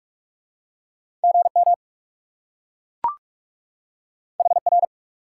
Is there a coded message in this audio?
Yes: Morse code from 1.2 to 4.8 s, heard as a keyed tone, about 40 wpm, 700 Hz. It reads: GM HR